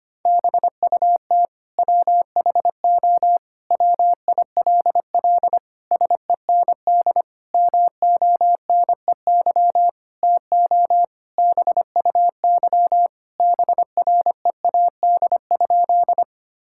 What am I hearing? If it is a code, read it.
Morse code, heard as a keyed tone, about 25 wpm, 700 Hz. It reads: BUT WHO WILL SEND MONEY TO BUY BREAD?
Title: Sentences from Top 1600 Words 25wpm